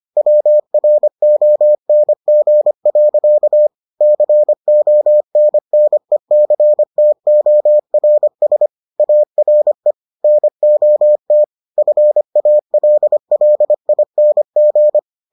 Morse code, heard as a keyed tone, about 25 wpm, 600 Hz. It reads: WRONG. CONNECTORS ARE NOT FALLING